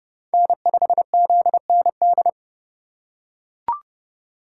Morse code, heard as a keyed tone, about 29 wpm, 700 Hz. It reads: N5ZND